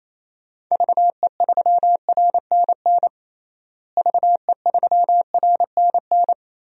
Morse code, heard as a keyed tone, about 27 wpm, 700 Hz. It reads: VE3RNN VE3RNN
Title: DXCC-100 Call Signs - 6 Characters - Repeated 2x - 28wpm